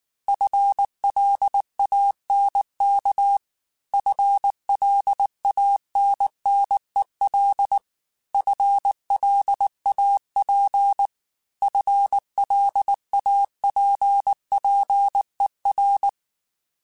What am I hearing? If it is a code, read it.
Morse code, heard as a keyed tone, about 19 wpm, 800 Hz. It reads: FLANK FLANNEL FLAP FLAPPER
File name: FL Runs / FL Runs mp3s / FL Runs @18